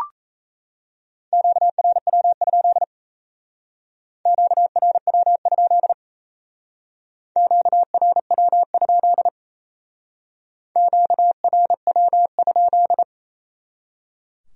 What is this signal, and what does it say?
Morse code, heard as a keyed tone, about 36 wpm, 700 Hz. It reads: QRW? QRW? QRW? QRW?